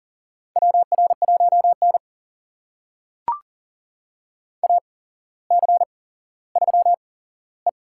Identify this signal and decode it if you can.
Morse code, heard as a keyed tone, about 39 wpm, spaced as for 17 wpm, 700 Hz. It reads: WR1N AC3E
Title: USA Call Signs - 4 Characters - 8x Character Spacing - 40wpm